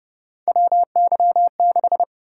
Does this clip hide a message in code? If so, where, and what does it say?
Yes: Morse code from 0.5 to 2.0 s, heard as a keyed tone, about 29 wpm, 700 Hz. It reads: WY6